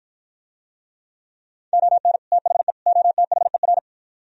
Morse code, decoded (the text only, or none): ON THE OTHER